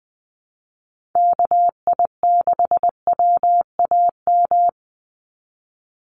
Morse code, heard as a keyed tone, about 20 wpm, 700 Hz. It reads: KI6WAM
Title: USA Call Signs 20wpm